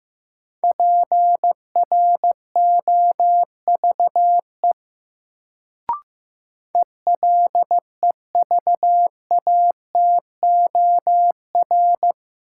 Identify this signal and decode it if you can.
Morse code, heard as a keyed tone, about 15 wpm, 700 Hz. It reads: PROVE ELEVATOR
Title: Sentences from Top 1600 Words - New Word Drill - 15wpm